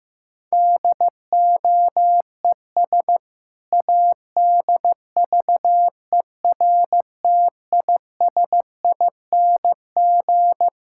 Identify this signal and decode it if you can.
Morse code, heard as a keyed tone, about 15 wpm, 700 Hz. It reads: DOES ADVERTISING
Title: Sentences from Top 1000 Words - Review Part 6 - 15wpm